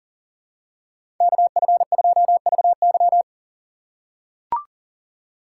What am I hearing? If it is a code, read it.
Morse code, heard as a keyed tone, about 37 wpm, 700 Hz. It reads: KF2VY
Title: USA Call Signs - 6x Character Spacing - 40wpm